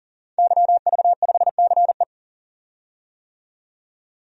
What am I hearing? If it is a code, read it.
Morse code, heard as a keyed tone, about 39 wpm, 700 Hz. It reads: YV5CE
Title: DXCC-100 Call Signs - 5 Characters - 40wpm